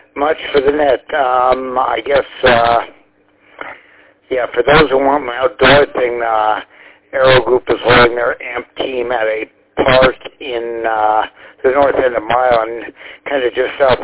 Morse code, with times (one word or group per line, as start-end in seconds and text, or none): none